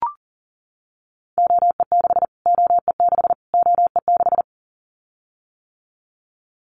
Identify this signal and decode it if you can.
Morse code, heard as a keyed tone, about 37 wpm, 700 Hz. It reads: OE6 OE6 OE6
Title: Callsign Prefixes for Top 25 DXCC - Repeated 3x - 40wpm